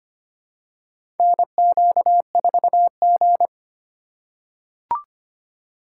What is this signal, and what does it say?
Morse code, heard as a keyed tone, about 25 wpm, 700 Hz. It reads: NQ4G
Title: USA Call Signs - 4 Characters - 2x Character Spacing - 25wpm